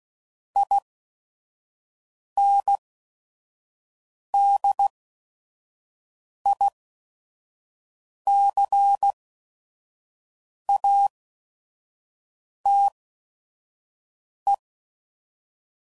Morse code, heard as a keyed tone, about 16 wpm, spaced as for 5 wpm, 800 Hz. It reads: INDICATE
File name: CW Code Groups @5WPM / Technical Subjects / Gates Intro